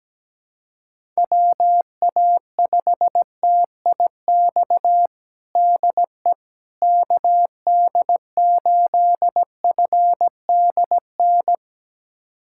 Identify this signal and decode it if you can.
Morse code, heard as a keyed tone, about 17 wpm, 700 Hz. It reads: WA5TIX DE KD8FDN
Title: USA Callsign DE Callsign - 6 Characters - 17wpm